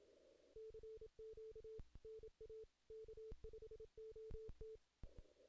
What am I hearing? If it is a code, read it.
Morse code, heard as a keyed tone, about 27 wpm, 440 Hz. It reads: CQ NA K5OT
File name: 2016 JAN CW NAQP / 2016 Jan naqp k5ot - 0-11